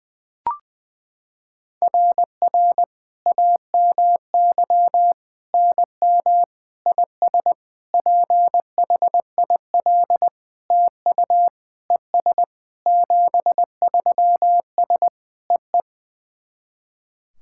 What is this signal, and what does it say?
Morse code, heard as a keyed tone, about 21 wpm, 700 Hz. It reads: RR AMY NM IS PHIL TU ES 73S EE
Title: SOTA QSO Practice 20wpm